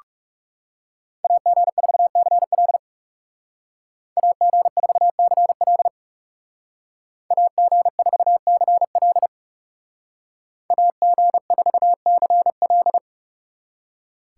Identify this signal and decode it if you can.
Morse code, heard as a keyed tone, about 37 wpm, 700 Hz. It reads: AG4CL AG4CL AG4CL AG4CL
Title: Speed Racing - USA 2x2 Call Signs 30wpm